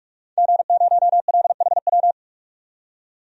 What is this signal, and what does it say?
Morse code, heard as a keyed tone, about 43 wpm, 700 Hz. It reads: G0LHW
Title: DXCC-100 Call Signs - 5 Characters - 45wpm